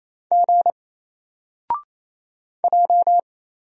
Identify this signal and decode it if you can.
Morse code, heard as a keyed tone, about 27 wpm, 700 Hz. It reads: G J
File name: Single-Letter-Number-28wpm